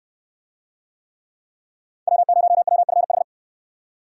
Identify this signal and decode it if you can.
Morse code, heard as a keyed tone, about 66 wpm, 700 Hz. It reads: W0WFH